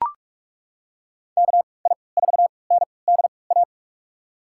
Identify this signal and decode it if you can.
Morse code, heard as a keyed tone, about 42 wpm, spaced as for 25 wpm, 700 Hz. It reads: KI4NDA
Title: USA Call Signs - 6 Characters - 3x Character Spacing - 45wpm